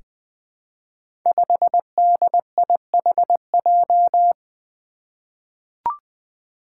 Morse code, heard as a keyed tone, about 20 wpm, 700 Hz. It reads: 5DIHJ
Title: Five Challenging Letter-Number Combinations 20wpm